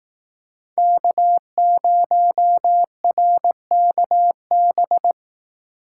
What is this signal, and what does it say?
Morse code, heard as a keyed tone, about 18 wpm, 700 Hz. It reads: K0RKB
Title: Speed Racing - USA 1x3 Call Signs 15wpm